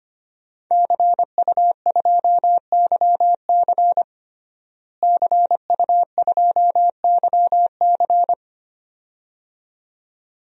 Morse code, heard as a keyed tone, about 25 wpm, 700 Hz. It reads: CU2YC CU2YC
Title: DXCC-100 Call Signs - 5 Characters - Repeated 2x - 25wpm